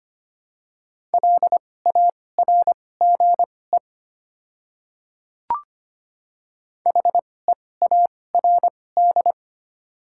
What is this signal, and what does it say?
Morse code, heard as a keyed tone, about 25 wpm, spaced as for 18 wpm, 700 Hz. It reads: LARGE HEARD
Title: 2x Character Spacing - Top 500 Words - 25wpm